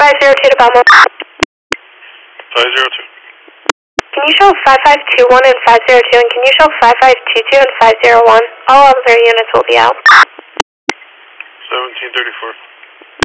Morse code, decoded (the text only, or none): none